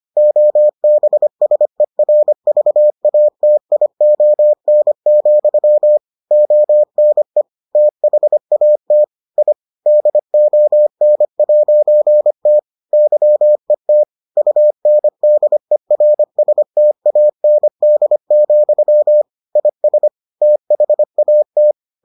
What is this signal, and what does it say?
Morse code, heard as a keyed tone, about 25 wpm, 600 Hz. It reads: OBSERVATION, ONE THAT I DON'T YET UNDERSTAND, IS THAT